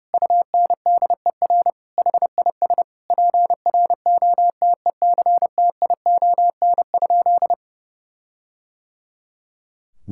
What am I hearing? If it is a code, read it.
Morse code, heard as a keyed tone, about 30 wpm, 700 Hz. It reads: UNDER HIS PROTECTION?
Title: Sentences from Top 1600 Words 30wpm